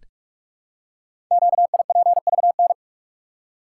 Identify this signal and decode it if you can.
Morse code, heard as a keyed tone, about 44 wpm, 700 Hz. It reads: QIPVN
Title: Code Groups - 5 Letter:Numbers - 45wpm